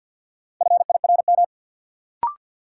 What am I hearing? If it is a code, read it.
Morse code, heard as a keyed tone, about 52 wpm, 700 Hz. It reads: FIRM
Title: Sentences from Top 1400 Words - New Word Drill - 50wpm